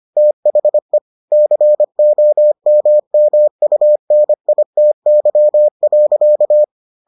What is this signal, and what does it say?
Morse code, heard as a keyed tone, about 25 wpm, 600 Hz. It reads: THE COMMUNITY.